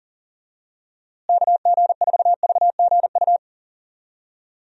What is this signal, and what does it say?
Morse code, heard as a keyed tone, about 40 wpm, 700 Hz. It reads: KG4VGU
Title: USA Call Signs - 6 Characters - Repeated 2x - 40wpm